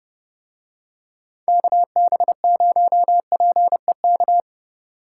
Morse code, heard as a keyed tone, about 31 wpm, 700 Hz. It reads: KB0PEK